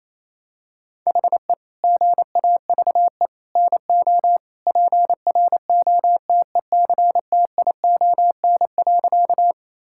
Morse code, heard as a keyed tone, about 28 wpm, 700 Hz. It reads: HE GAVE NO PROTECTION.
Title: Sentences from Top 1600 Words 28wpm